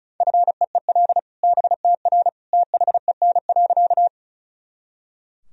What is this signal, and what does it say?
Morse code, heard as a keyed tone, about 36 wpm, 700 Hz. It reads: FEEL BTR THEN.
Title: Sentences from Top 400 Words and Abbreviations 35wpm